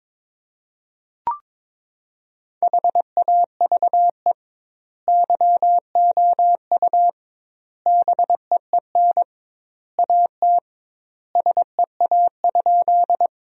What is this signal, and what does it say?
Morse code, heard as a keyed tone, about 22 wpm, 700 Hz. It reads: HAVE YOU BEEN AT SEA?
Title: 2x Word-Spacing - Sentences from Top 300 Words - 22wpm